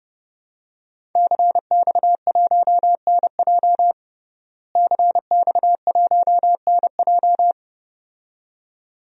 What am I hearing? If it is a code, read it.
Morse code, heard as a keyed tone, about 32 wpm, 700 Hz. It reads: CX1NJ CX1NJ
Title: DXCC-100 Call Signs - 5 Characters - Repeated 2x - 30wpm